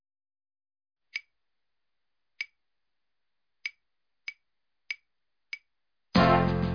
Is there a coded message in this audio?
No.